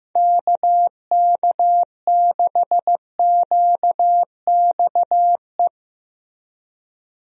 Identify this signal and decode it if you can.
Morse code, heard as a keyed tone, about 15 wpm, 700 Hz. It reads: KK6QXE
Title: USA Call Signs - 6 Characters - Repeated 2x - 15wpm